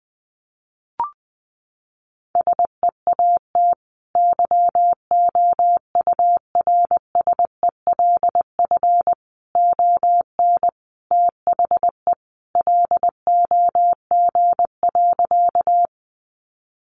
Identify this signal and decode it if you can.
Morse code, heard as a keyed tone, about 20 wpm, 700 Hz. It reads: SEAT YOURSELF ON THE LOG.
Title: Sentences from Top 1500 Words 20wpm